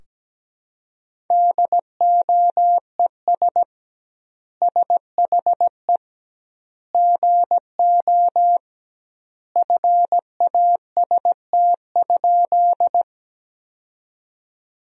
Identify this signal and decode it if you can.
Morse code, heard as a keyed tone, about 17 wpm, 700 Hz. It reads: DOES SHE GO FAST?
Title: 2x Word-Spacing - Sentences from Top 400 Words - 17wpm